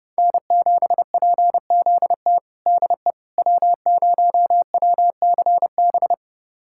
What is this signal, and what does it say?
Morse code, heard as a keyed tone, about 30 wpm, 700 Hz. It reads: N7PZT DE W0WCB
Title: USA Callsign DE Callsign 30wpm